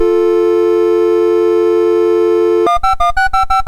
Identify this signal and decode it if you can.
Dial tone (North America), 350 + 440 Hz, with noise.
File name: Misc / SoundEffects / Dial Tone T001.mus MOS6581R2